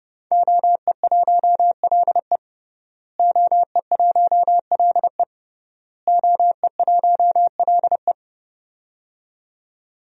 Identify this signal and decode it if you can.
Morse code, heard as a keyed tone, about 30 wpm, 700 Hz. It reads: OE1LE OE1LE OE1LE
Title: DXCC-100 Call Signs - 5 Characters - Repeated 3x - 30wpm